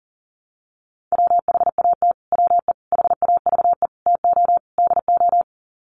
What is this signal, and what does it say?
Morse code, heard as a keyed tone, about 40 wpm, 700 Hz. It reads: WHAT WE HAVE TO DO
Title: Sets of 5 Words Encore 40wpm